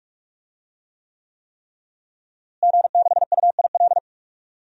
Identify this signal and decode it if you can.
Morse code, heard as a keyed tone, about 45 wpm, 700 Hz. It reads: G6UIL
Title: DXCC-100 Call Signs - 5 Characters - Repeated 3x - 45wpm